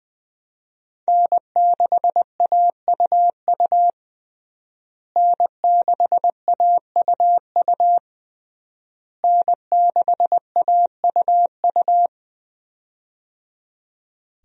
Morse code, heard as a keyed tone, about 20 wpm, 700 Hz. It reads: N6AUU N6AUU N6AUU
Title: USA Call Signs - 5 Characters - Repeated 3x - 20wpm